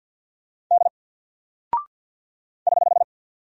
Morse code, heard as a keyed tone, about 49 wpm, 700 Hz. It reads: D <HH>
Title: Single Character (Letter:Number:Punctuation:Prosign) Alt 50wpm